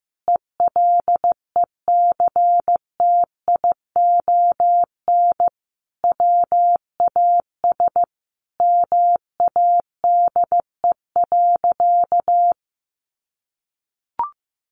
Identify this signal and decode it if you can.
Morse code, heard as a keyed tone, about 15 wpm, 700 Hz. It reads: ELECTION WAS MADE.
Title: Sentences from Top 1500 Words 15wpm